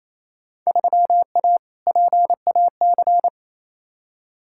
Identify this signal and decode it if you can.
Morse code, heard as a keyed tone, about 28 wpm, 700 Hz. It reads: 3A PAC
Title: Field Day Exchanges Encore - 28wpm